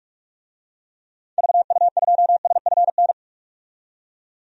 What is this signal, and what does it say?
Morse code, heard as a keyed tone, about 46 wpm, 700 Hz. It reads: VU2SFN
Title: DXCC-100 Call Signs - 6 Characters - 45wpm